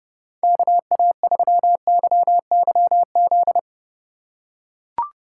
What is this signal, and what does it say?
Morse code, heard as a keyed tone, about 31 wpm, 700 Hz. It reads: KA3YYZ
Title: USA Call Signs - 6 Characters - 3x Character Spacing - 30wpm